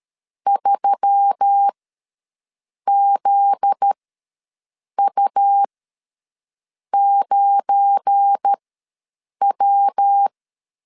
Morse code, heard as a keyed tone, about 13 wpm, spaced as for 6 wpm, 800 Hz. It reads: 3ZU9W
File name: LEZIONI / 7 GF9Z / 7 GF9Z ws 12wpm 3321